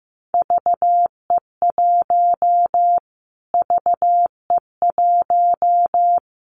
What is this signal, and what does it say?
Morse code, heard as a keyed tone, about 15 wpm, 700 Hz. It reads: VE1 VE1
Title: Callsign Prefixes for Top 25 DXCC - Repeated 3x - 15wpm